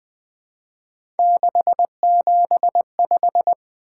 Morse code, heard as a keyed tone, about 20 wpm, 700 Hz. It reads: 675
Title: Three Letter-Number Combinations - 20wpm